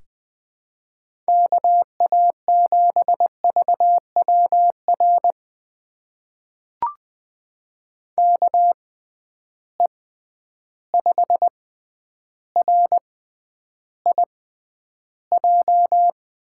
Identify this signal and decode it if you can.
Morse code, heard as a keyed tone, about 20 wpm, spaced as for 10 wpm, 700 Hz. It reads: KA7VWR KE5RIJ
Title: USA Call Signs - 6 Characters - 6x Character Spacing - 20wpm